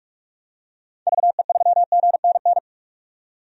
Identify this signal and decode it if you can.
Morse code, heard as a keyed tone, about 46 wpm, 700 Hz. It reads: VE3GNN
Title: DXCC-100 Call Signs - 6 Characters - 45wpm